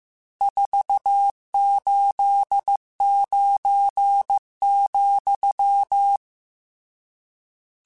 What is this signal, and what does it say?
Morse code, heard as a keyed tone, about 15 wpm, 800 Hz. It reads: 489,
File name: RSTs / RST-mp3s / Various RST @13